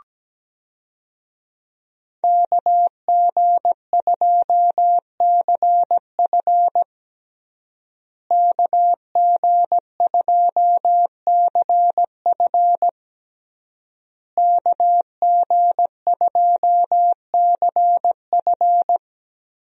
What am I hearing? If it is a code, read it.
Morse code, heard as a keyed tone, about 17 wpm, 700 Hz. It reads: KG2CF KG2CF KG2CF
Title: USA Call Signs - 5 Characters - Repeated 3x - 17wpm